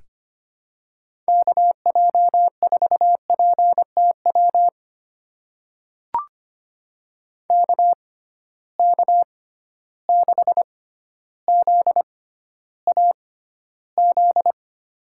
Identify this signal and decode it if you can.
Morse code, heard as a keyed tone, about 25 wpm, spaced as for 13 wpm, 700 Hz. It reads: KJ4PTW KK6ZAZ
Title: USA Call Signs - 6 Characters - 6x Character Spacing - 25wpm